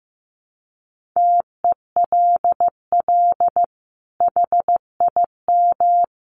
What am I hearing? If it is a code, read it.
Morse code, heard as a keyed tone, about 15 wpm, 700 Hz. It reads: TELL HIM
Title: Sentences from Top 1200 Words 15wpm